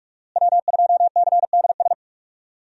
Morse code, heard as a keyed tone, about 42 wpm, 700 Hz. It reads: W2CDS
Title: USA Call Signs - 5 Characters - Repeated 3x - 45wpm